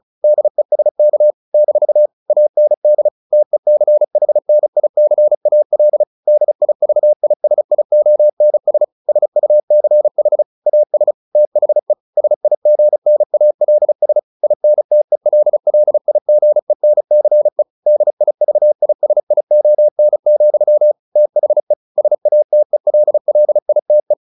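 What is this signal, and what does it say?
Morse code, heard as a keyed tone, about 34 wpm, 600 Hz. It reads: DESK - AND TECHNICAL DIVISIONS SUCH AS THE SIGNALS INTELLIGENCE DIVISION, THE SATELLITE
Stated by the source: Morse practice at 35 wpm from CW med Gnister, sending Ken Follett's novel Never in English.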